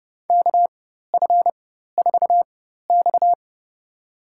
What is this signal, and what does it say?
Morse code, heard as a keyed tone, about 32 wpm, spaced as for 14 wpm, 700 Hz. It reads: KF4X
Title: USA Call Signs - 4 Characters - 4x Character Spacing - 30wpm